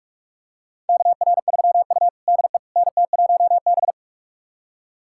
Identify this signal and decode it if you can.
Morse code, heard as a keyed tone, about 42 wpm, 700 Hz. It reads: KR3U DE NT1B
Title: USA Callsign DE Callsign - 4 Characters - 45wpm